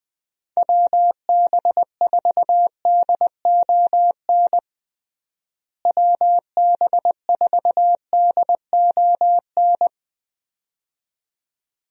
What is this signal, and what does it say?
Morse code, heard as a keyed tone, about 21 wpm, 700 Hz. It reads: WB4DON WB4DON
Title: USA Call Signs - 6 Characters - Repeated 2x - 20wpm